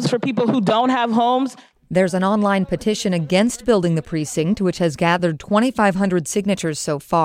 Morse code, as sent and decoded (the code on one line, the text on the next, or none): none